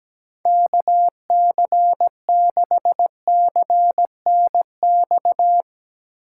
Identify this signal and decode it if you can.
Morse code, heard as a keyed tone, about 17 wpm, 700 Hz. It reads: KC6CNX
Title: USA Call Signs - 6 Characters - 2x Character Spacing - 17wpm